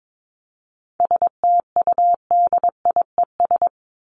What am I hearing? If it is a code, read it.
Morse code, heard as a keyed tone, about 22 wpm, 700 Hz. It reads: STUDIES
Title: ES:S Suffix - Top 100 Words with Base Words 22wpm